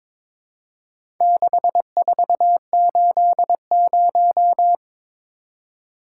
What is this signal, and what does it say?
Morse code, heard as a keyed tone, about 22 wpm, 700 Hz. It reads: 6480
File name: practice / 22wpm / Four Letter-Number Combinations - 22wpm